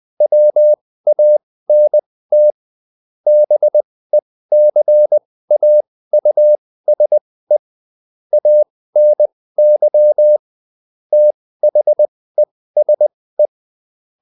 Morse code, heard as a keyed tone, about 20 wpm, spaced as for 15 wpm, 600 Hz. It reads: WANT BECAUSE ANY THESE
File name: Top-100-Words-15